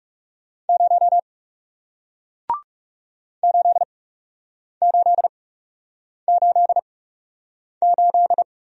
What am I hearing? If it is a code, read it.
Morse code, heard as a keyed tone, about 40 wpm, spaced as for 8 wpm, 700 Hz. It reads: 0 8888